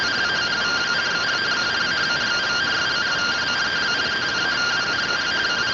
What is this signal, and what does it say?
Two-tone teleprinter signal, about 170 Hz shift.